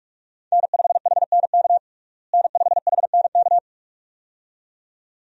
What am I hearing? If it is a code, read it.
Morse code, heard as a keyed tone, about 43 wpm, 700 Hz. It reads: N5HNK N5HNK
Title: USA Call Signs - 5 Characters - Repeated 2x - 45wpm